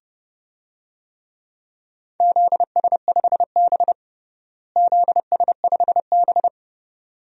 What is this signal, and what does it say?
Morse code, heard as a keyed tone, about 30 wpm, 700 Hz. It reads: ZS5B ZS5B
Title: DXCC-100 Call Signs - 4 Characters - Repeated 2x - 30wpm